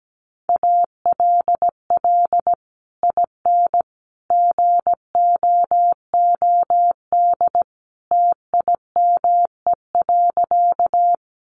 Morse code, heard as a keyed tone, about 17 wpm, 700 Hz. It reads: ALL IN GOOD TIME.
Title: Sentences from Top 200 Words 17wpm